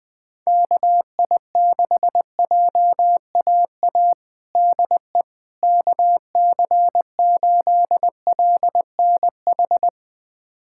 Morse code, heard as a keyed tone, about 21 wpm, 700 Hz. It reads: KI6JAA DE KC8LNH
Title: USA Callsign DE Callsign 20wpm